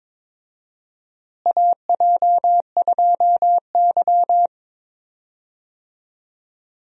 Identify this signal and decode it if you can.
Morse code, heard as a keyed tone, about 22 wpm, 700 Hz. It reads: AJ2Y
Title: USA Call Signs - 4 Characters - Repeated 3x - 22wpm